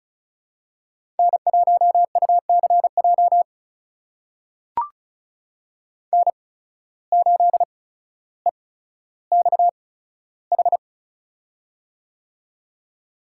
Morse code, heard as a keyed tone, about 36 wpm, spaced as for 15 wpm, 700 Hz. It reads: N1UCJ N8EXH
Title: USA Call Signs - 5 Characters - 8x Character Spacing - 35wpm